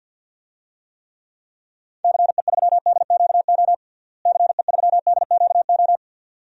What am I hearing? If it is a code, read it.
Morse code, heard as a keyed tone, about 52 wpm, 700 Hz. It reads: CE3DQO CE3DQO
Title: DXCC-100 Call Signs - 6 Characters - Repeated 2x - 50wpm